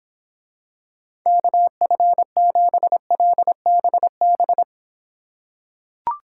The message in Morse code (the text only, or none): KF7LBB